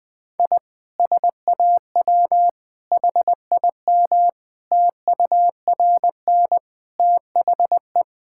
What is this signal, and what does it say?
Morse code, heard as a keyed tone, about 20 wpm, 700 Hz. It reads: I SAW HIM TURN THE
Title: Sentences from Top 1600 Words 20wpm